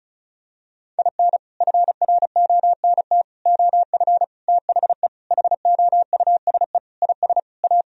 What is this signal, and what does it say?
Morse code, heard as a keyed tone, about 36 wpm, 700 Hz. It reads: IN FRONT OF THE HOUSE IS A
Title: Sentences from Top 1000 Words - Review Part 4 - 35wpm